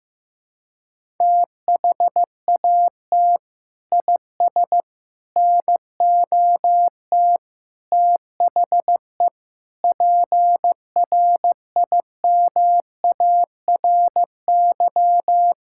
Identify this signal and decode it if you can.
Morse code, heard as a keyed tone, about 15 wpm, 700 Hz. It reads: THAT IS NOT THE PRIMARY
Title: Sentences from Top 1000 Words - Review Part 5 - 15wpm